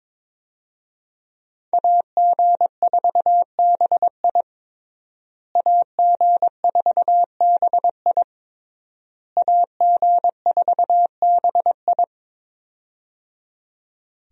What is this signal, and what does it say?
Morse code, heard as a keyed tone, about 21 wpm, 700 Hz. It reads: AG4BI AG4BI AG4BI
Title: USA Call Signs - 5 Characters - Repeated 3x - 22wpm